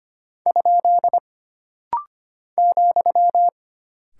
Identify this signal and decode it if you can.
Morse code, heard as a keyed tone, about 25 wpm, 700 Hz. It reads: ? ,
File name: Single Punctuation-Prosign - Rapid-Fire - 25wpm